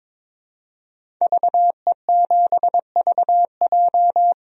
Morse code, heard as a keyed tone, about 22 wpm, 700 Hz. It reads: VE7VJ
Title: DXCC-100 Call Signs - 5 Characters - 22wpm